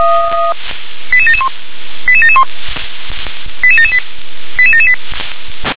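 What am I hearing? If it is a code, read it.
Five-tone selective call (ZVEI), heard as quick runs of five tones, tones of about 70 ms. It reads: 88771 88771 88778 88778
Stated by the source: ZVEI five-tone selective call received by SDR.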